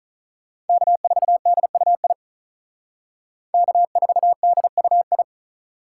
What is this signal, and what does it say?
Morse code, heard as a keyed tone, about 46 wpm, 700 Hz. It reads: K4DUI K4DUI